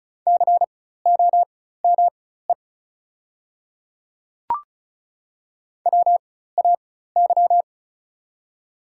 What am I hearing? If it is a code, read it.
Morse code, heard as a keyed tone, about 35 wpm, spaced as for 16 wpm, 700 Hz. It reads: COME WAY
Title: 4x Character Spacing - Top 100 Words - 35wpm